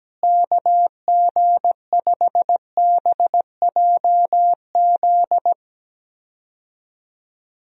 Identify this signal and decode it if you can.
Morse code, heard as a keyed tone, about 17 wpm, 700 Hz. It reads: KG5BJZ
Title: USA Call Signs - 6 Characters - Repeated 2x - 17wpm